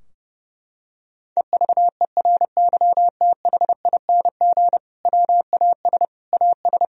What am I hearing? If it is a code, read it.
Morse code, heard as a keyed tone, about 30 wpm, 700 Hz. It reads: EVERYTHING WAS AS
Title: Sentences from Top 1000 Words - Review Part 2 - 30wpm